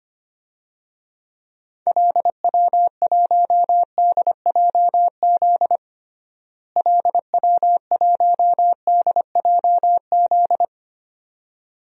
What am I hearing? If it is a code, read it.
Morse code, heard as a keyed tone, about 25 wpm, 700 Hz. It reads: LW1DJZ LW1DJZ
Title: DXCC-100 Call Signs - 6 Characters - Repeated 2x - 25wpm